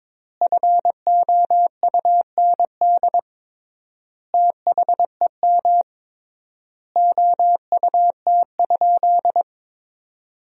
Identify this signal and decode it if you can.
Morse code, heard as a keyed tone, about 22 wpm, 700 Hz. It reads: FOUND THEM OUT?
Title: 3x Word-Spacing - Sentences from Top 300 Words - 22wpm